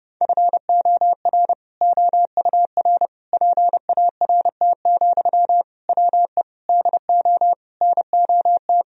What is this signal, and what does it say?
Morse code, heard as a keyed tone, about 30 wpm, 700 Hz. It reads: FOR OUR PART, WE DO NOT
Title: Sentences from Top 1000 Words - Review Part 1 - 30wpm